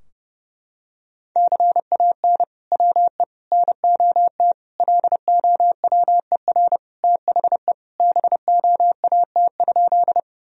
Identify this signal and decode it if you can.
Morse code, heard as a keyed tone, about 30 wpm, 700 Hz. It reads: CAN WE NOT LOWER THE BOAT?
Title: Sentences from Top 1000 Words - Review Part 4 - 30wpm